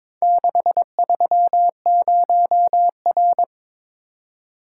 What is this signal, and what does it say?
Morse code, heard as a keyed tone, about 22 wpm, 700 Hz. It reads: 630R
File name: practice / 22wpm / Four Letter-Number Combinations - 22wpm